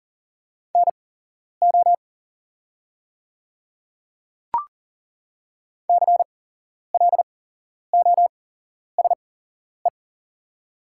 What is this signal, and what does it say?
Morse code, heard as a keyed tone, about 38 wpm, spaced as for 11 wpm, 700 Hz. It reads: NO CLOSE
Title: 8x Character Spacing - Top 300 Words - 40wpm